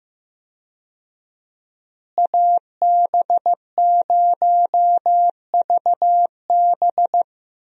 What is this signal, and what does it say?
Morse code, heard as a keyed tone, about 15 wpm, 700 Hz. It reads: AB0VB
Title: USA Call Signs - 5 Characters - Repeated 2x - 15wpm